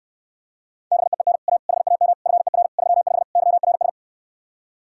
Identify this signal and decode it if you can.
Morse code, heard as a keyed tone, about 69 wpm, 700 Hz. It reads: BEEN A HAM FER 35 YRS